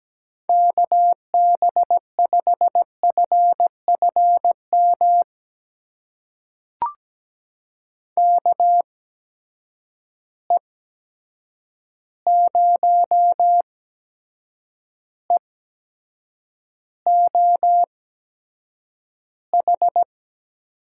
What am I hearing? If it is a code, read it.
Morse code, heard as a keyed tone, about 17 wpm, spaced as for 7 wpm, 700 Hz. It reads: KB5FFM KE0EOH
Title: USA Call Signs - 6 Characters - 8x Character Spacing - 17wpm